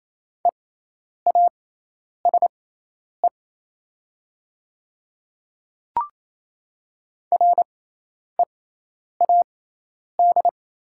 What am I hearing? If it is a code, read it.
Morse code, heard as a keyed tone, about 28 wpm, spaced as for 10 wpm, 700 Hz. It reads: EASE READ
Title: 6x Character Spacing - Top 500 Words - 28wpm